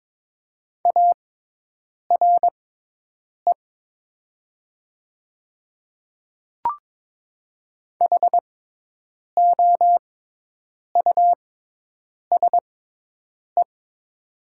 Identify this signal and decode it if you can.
Morse code, heard as a keyed tone, about 22 wpm, spaced as for 8 wpm, 700 Hz. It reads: ARE HOUSE
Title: 6x Character Spacing - Top 300 Words - 22wpm